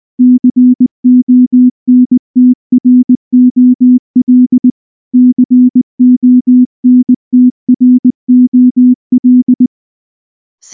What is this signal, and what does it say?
Morse code, heard as a keyed tone, about 19 wpm, 250 Hz. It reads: CONTROL CONTROL
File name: morse / morse-training-20WPM-2